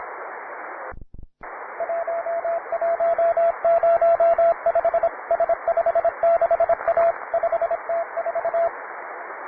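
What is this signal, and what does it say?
Morse code, heard as a keyed tone, about 25 wpm, 670 Hz, with radio noise. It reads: 1105 S56A 5T4
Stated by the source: Morse code QTC list traffic from the Worked All Europe (WAE) contest, advanced level.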